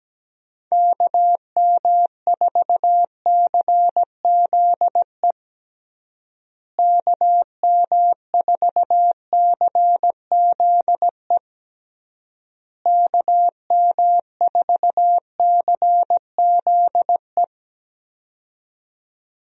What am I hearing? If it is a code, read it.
Morse code, heard as a keyed tone, about 17 wpm, 700 Hz. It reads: KM4CZE KM4CZE KM4CZE
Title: USA Call Signs - 6 Characters - Repeated 3x - 17wpm